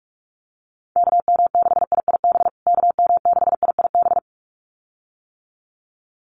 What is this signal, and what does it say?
Morse code, heard as a keyed tone, about 44 wpm, 700 Hz. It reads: KM6IIB KM6IIB
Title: USA Call Signs Repeated 2x 45wpm